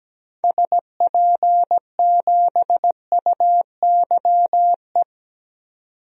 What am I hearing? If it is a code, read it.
Morse code, heard as a keyed tone, about 17 wpm, 700 Hz. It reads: SP7UYE
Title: DXCC-100 Call Signs - 6 Characters - 17wpm